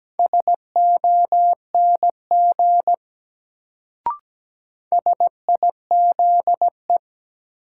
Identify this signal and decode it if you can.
Morse code, heard as a keyed tone, about 17 wpm, 700 Hz. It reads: SONG SIZE
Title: Sentences from Top 400 Words - New Word Drill - 17wpm